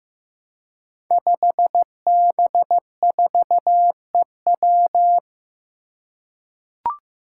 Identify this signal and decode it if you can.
Morse code, heard as a keyed tone, about 15 wpm, 700 Hz. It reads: 5B4EW
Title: Five Challenging Letter-Number Combinations 15wpm